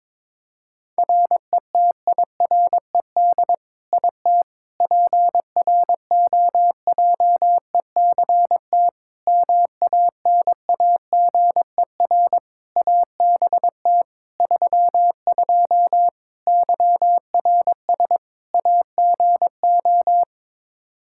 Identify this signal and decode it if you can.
Morse code, heard as a keyed tone, about 22 wpm, 700 Hz. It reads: RETIRED IT PROJECT MANAGER ABT 32 YRS AGO
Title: Practice QSO - Exchange 3 - More Info - 22wpm